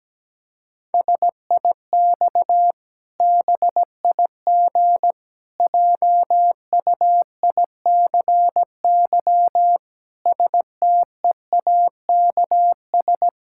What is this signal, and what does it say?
Morse code, heard as a keyed tone, about 17 wpm, 700 Hz. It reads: SIX BIG JUICY STEAKS
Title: Pangram Challenge v2 - 17wpm